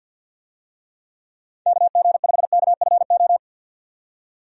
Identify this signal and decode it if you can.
Morse code, heard as a keyed tone, about 50 wpm, 700 Hz. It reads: KG5KFO
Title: USA Call Signs - 6 Characters - 50wpm